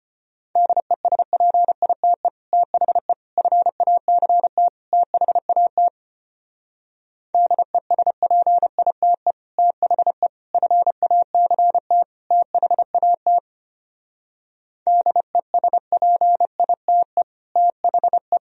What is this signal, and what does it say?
Morse code, heard as a keyed tone, about 32 wpm, 700 Hz. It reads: DESPITE THE FACT THAT DESPITE THE FACT THAT DESPITE THE